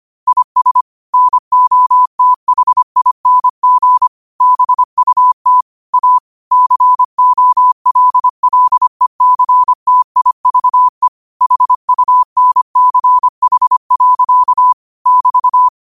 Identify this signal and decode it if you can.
Morse code, heard as a keyed tone, about 25 wpm, 1000 Hz. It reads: IS NOTHING BUT A COLLECTIVE HUNCH. <BT>
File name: qotdfiles / 25wpm